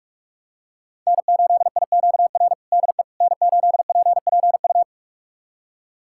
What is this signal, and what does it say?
Morse code, heard as a keyed tone, about 42 wpm, 700 Hz. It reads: N8IQR DE N8PPU
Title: USA Callsign DE Callsign - 5 Characters - 45wpm